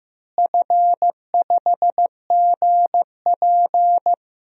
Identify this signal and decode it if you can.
Morse code, heard as a keyed tone, about 15 wpm, 700 Hz. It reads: F5GP
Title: DXCC-100 Call Signs - 4 Characters - Repeated 2x - 15wpm